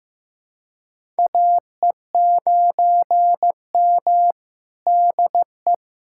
Morse code, heard as a keyed tone, about 15 wpm, 700 Hz. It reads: AE9M DE
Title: USA Callsign DE Callsign - 4 Characters - 15wpm